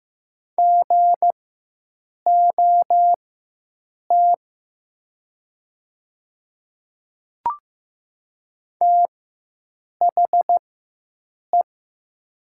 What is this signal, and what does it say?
Morse code, heard as a keyed tone, about 15 wpm, spaced as for 7 wpm, 700 Hz. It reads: GOT THE
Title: 4x Character Spacing - Top 500 Words - 15wpm